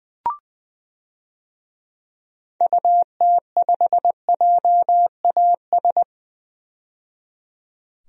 Morse code, heard as a keyed tone, about 21 wpm, 700 Hz. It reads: UT5JAS
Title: DXCC-100 Call Signs - 6 Characters - 20wpm